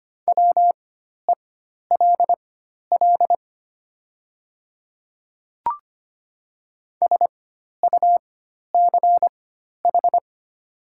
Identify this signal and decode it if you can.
Morse code, heard as a keyed tone, about 25 wpm, spaced as for 12 wpm, 700 Hz. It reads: WELL SUCH
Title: 4x Character Spacing - Top 200 Words - 25wpm